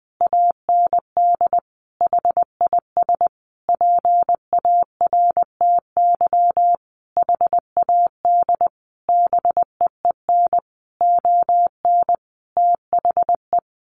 Morse code, heard as a keyed tone, about 20 wpm, 700 Hz. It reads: AND HIS PARTY HAD BEEN ON THE
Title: Sentences from Top 1400 Words 20wpm